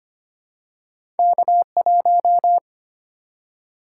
Morse code, heard as a keyed tone, about 25 wpm, 700 Hz. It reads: K1
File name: Two Letter-Number Combinations - 25wpm